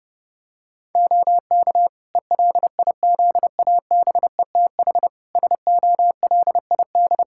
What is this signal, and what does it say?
Morse code, heard as a keyed tone, about 30 wpm, 700 Hz. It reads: OK ELIZABETH SOLID
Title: Practice QSO - Exchange 3 - More Info - 30wpm